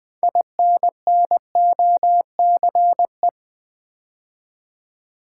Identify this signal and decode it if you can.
Morse code, heard as a keyed tone, about 20 wpm, 700 Hz. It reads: INNOCE
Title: Build A Word - 2 Letters At A Time - 8-Letter Words - 20wpm